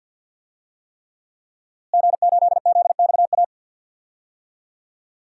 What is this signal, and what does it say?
Morse code, heard as a keyed tone, about 45 wpm, 700 Hz. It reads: G8ZXA
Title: DXCC-100 Call Signs - 5 Characters - Repeated 3x - 50wpm